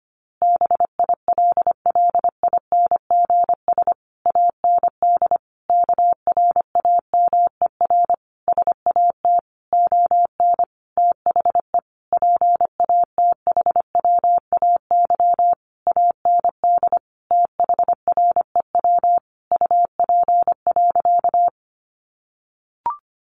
Morse code, heard as a keyed tone, about 25 wpm, 700 Hz. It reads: BILLINGS AND KRAMER SAT ON THE PATHWAY AND THREW UP.
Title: A Sound Of Thunder 25wpm